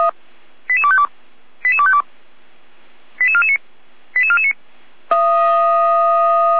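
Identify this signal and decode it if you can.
Five-tone selective call (ZVEI), heard as quick runs of five tones, tones of about 70 ms. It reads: 88252 88252 88449 88449
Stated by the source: Received by SDR and recorded by the Feuerwehr Orlamünde fire brigade.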